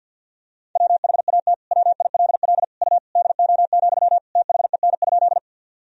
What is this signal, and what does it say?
Morse code, heard as a keyed tone, about 48 wpm, 700 Hz. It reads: WHAT WILL U DO, THEN?
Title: Sentences from Top 100 Words and Abbreviations 50wpm